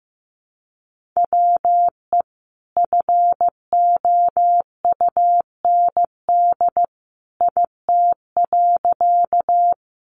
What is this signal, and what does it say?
Morse code, heard as a keyed tone, about 15 wpm, 700 Hz. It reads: WE FOUND IT.
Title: Sentences from Top 300 Words 15wpm